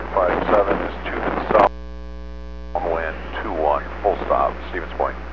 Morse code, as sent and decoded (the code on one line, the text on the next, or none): none